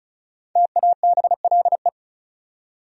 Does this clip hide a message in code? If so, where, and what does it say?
Yes: Morse code from 0.6 to 1.9 s, heard as a keyed tone, about 45 wpm, 700 Hz. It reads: TABLE